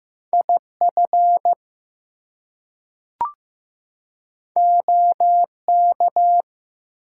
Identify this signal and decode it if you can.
Morse code, heard as a keyed tone, about 15 wpm, 700 Hz. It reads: IF OK
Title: Two Letter Words - 15wpm